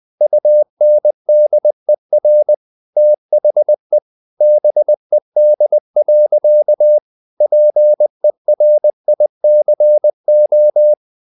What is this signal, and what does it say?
Morse code, heard as a keyed tone, about 20 wpm, 600 Hz. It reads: UNDER THE BED. PERICO
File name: the old man and the sea / the old man and the sea part 1 20wpm